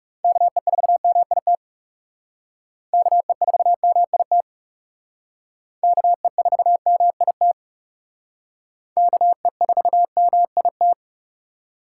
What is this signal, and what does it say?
Morse code, heard as a keyed tone, about 40 wpm, 700 Hz. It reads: KE4MIT KE4MIT KE4MIT KE4MIT